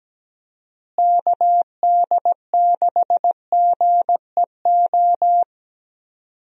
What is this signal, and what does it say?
Morse code, heard as a keyed tone, about 17 wpm, 700 Hz. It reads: KD6GEO
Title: USA Call Signs - 6 Characters - Repeated 2x - 17wpm